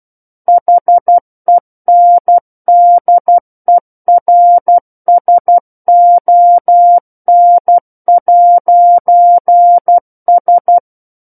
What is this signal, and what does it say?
Morse code, heard as a keyed tone, about 12 wpm, 700 Hz. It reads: HENDERSON'S
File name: Book One / 03-Chapter 3 - 12 WPM